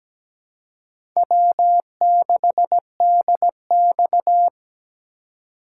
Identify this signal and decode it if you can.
Morse code, heard as a keyed tone, about 17 wpm, 700 Hz. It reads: W6DX
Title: USA Call Signs - 4 Characters - 2x Character Spacing - 17wpm